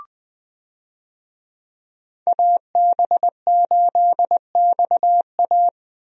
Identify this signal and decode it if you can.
Morse code, heard as a keyed tone, about 20 wpm, 700 Hz. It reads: AB8XA
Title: USA Call Signs - 5 Characters - 20wpm